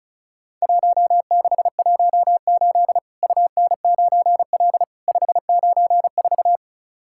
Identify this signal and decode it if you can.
Morse code, heard as a keyed tone, about 36 wpm, 700 Hz. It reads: 1618 UN9L 594
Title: Worked All Europe DX Contest - QTC Exchange - 35wpm